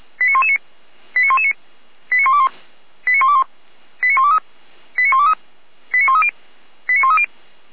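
Five-tone selective call (ZVEI), heard as quick runs of five tones, tones of about 70 ms. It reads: 89119 89119 89121 89121 89123 89123 89130 89130
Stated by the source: Five-tone ZVEI selective call received by SDR, from the Feuerwehr Orlamünde fire brigade server.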